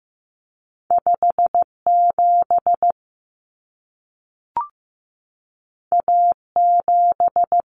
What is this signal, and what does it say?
Morse code, heard as a keyed tone, about 15 wpm, 700 Hz. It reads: 57 A7
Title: Common Cut Numbers - 2-Digit 15wpm